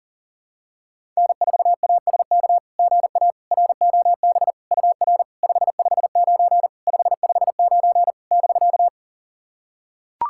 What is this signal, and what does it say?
Morse code, heard as a keyed tone, about 40 wpm, 690 Hz. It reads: N4ASK GA ROB UR 559 559 <BK>
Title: SOTA QSO Practice 40wpm